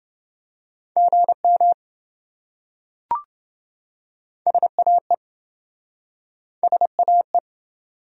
Morse code, heard as a keyed tone, about 28 wpm, 700 Hz. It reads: GM SAE SAE